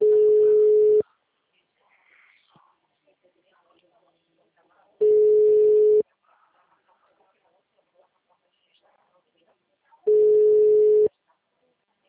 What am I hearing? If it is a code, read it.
Ringback tone (Europe, 425 Hz), with speech.
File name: out- -2009- - - .2